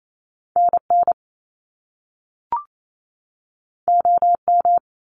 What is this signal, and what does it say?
Morse code, heard as a keyed tone, about 28 wpm, 700 Hz. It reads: NN OM
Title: Top Two Letter Combinations in Words 28wpm